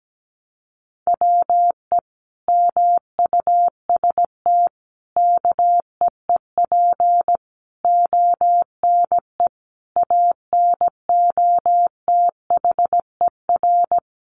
Morse code, heard as a keyed tone, about 17 wpm, 700 Hz. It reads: WE MUST KEEP ONE ANOTHER
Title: Sentences from Top 600 Words 17wpm